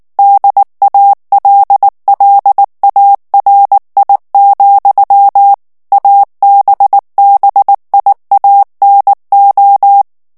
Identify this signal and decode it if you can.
Morse code, heard as a keyed tone, about 19 wpm, 800 Hz. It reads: DALLARI, ABBIANO
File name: morse mp3 / ITALIEN texte clair 2 vitesse 18 mots minute